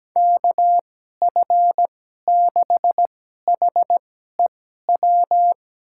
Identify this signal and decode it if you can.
Morse code, heard as a keyed tone, about 17 wpm, spaced as for 12 wpm, 700 Hz. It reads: KF6HEW
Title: USA Call Signs - 6 Characters - 2x Character Spacing - 17wpm